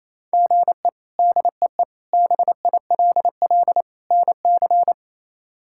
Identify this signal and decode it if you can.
Morse code, heard as a keyed tone, about 28 wpm, 700 Hz. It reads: GE DEE BILL NC
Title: K1USN SST QSO Practice - 28wpm